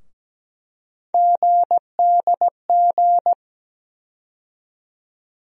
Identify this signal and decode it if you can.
Morse code, heard as a keyed tone, about 17 wpm, 700 Hz. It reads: GDG
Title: USA Call Sign Postfixes - 3 Letters 17wpm